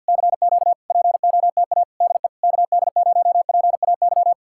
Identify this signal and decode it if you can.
Morse code, heard as a keyed tone, about 66 wpm, spaced as for 57 wpm, 700 Hz. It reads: CQ POTA DE KD0PAY
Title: Speed-Racing Common QSO Phrases - 50wpm